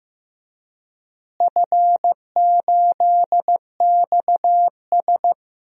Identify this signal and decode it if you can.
Morse code, heard as a keyed tone, about 15 wpm, 700 Hz. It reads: F8XS
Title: DXCC-100 Call Signs - 4 Characters - Repeated 3x - 15wpm